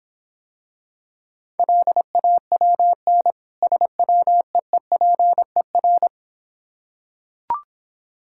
Morse code, heard as a keyed tone, about 26 wpm, 690 Hz. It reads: LAWN SWEEPER